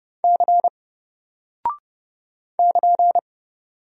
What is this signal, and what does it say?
Morse code, heard as a keyed tone, about 30 wpm, 700 Hz. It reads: C <KN>
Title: Single Character (Letter:Number:Punctuation:Prosign) 30wpm